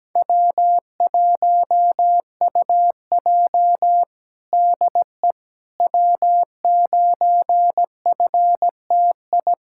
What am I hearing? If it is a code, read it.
Morse code, heard as a keyed tone, about 17 wpm, 700 Hz. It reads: W1UJ DE W9FTI
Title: Practice QSO - Exchange 4 - Ending - 17wpm